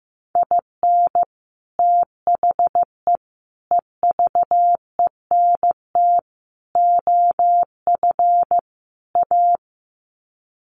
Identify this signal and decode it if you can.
Morse code, heard as a keyed tone, about 15 wpm, 700 Hz. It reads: IN THE EVENT OF A
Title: Sets of 5 Words Encore 15wpm